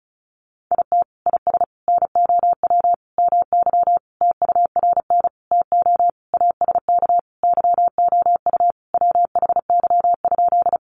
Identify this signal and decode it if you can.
Morse code, heard as a keyed tone, about 34 wpm, 700 Hz. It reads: IT IS NOW MY TURN TO ASK YOU WHY?
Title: Sentences from Top 200 Words 35wpm